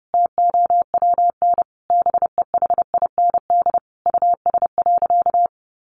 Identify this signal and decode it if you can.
Morse code, heard as a keyed tone, about 30 wpm, 700 Hz. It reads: TOWN BEHIND US.
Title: Sentences from Top 1300 Words 30wpm